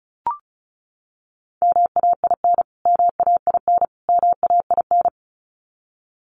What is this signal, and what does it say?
Morse code, heard as a keyed tone, about 34 wpm, 700 Hz. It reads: MAIN MAIN MAIN
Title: Top 500 Words Repeated 35wpm